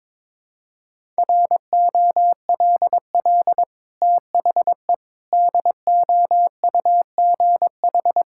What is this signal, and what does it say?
Morse code, heard as a keyed tone, about 22 wpm, 700 Hz. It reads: ROLL THE DOUGH
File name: practice / 22wpm / Easy Sentences with Hints - 22wpm